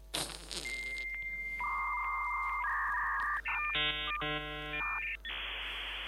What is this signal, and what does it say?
Dial-up modem handshake.